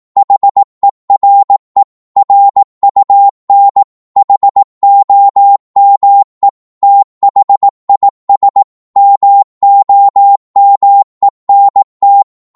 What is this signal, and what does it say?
Morse code, heard as a keyed tone, about 18 wpm, 800 Hz. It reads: HERE RUN HOME THIS MOMENT